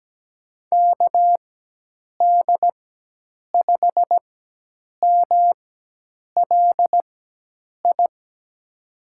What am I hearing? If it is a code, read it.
Morse code, heard as a keyed tone, about 17 wpm, spaced as for 8 wpm, 700 Hz. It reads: KD5MLI
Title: USA Call Signs - 4x Character Spacing - 17wpm